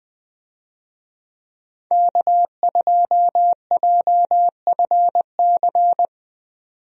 Morse code, heard as a keyed tone, about 20 wpm, 700 Hz. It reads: K2JFC
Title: USA Call Signs - 5 Characters - 20wpm